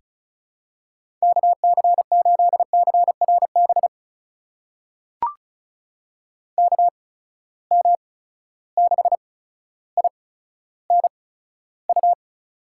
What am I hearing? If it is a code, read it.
Morse code, heard as a keyed tone, about 36 wpm, spaced as for 15 wpm, 700 Hz. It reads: KC8CRB KM6INU
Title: USA Call Signs - 8x Character Spacing - 35wpm